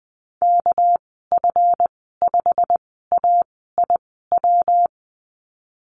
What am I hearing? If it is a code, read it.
Morse code, heard as a keyed tone, about 20 wpm, spaced as for 14 wpm, 700 Hz. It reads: KF5AIW
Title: 213 Lesson 34g - Number 8 - Call Signs 20wpm